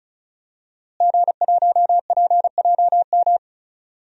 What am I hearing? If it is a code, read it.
Morse code, heard as a keyed tone, about 34 wpm, 700 Hz. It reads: G1PJM
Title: DXCC-100 Call Signs - 5 Characters - Repeated 3x - 35wpm